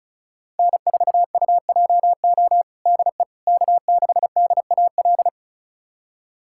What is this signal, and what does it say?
Morse code, heard as a keyed tone, about 36 wpm, 700 Hz. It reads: N4UJO DE K6DAL
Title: USA Callsign DE Callsign - 5 Characters - 35wpm